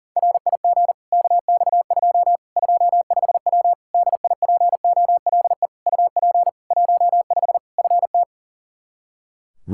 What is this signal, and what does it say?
Morse code, heard as a keyed tone, about 37 wpm, 700 Hz. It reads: RIG KX2 25W DIPOLE UP 15 FT
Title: Practice QSO - Exchange 2 - Info - 40wpm